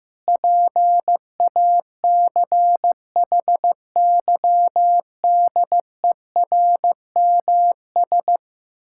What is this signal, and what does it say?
Morse code, heard as a keyed tone, about 15 wpm, 700 Hz. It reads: PACHYDERMS
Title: Pangram Challenge v2 - 15wpm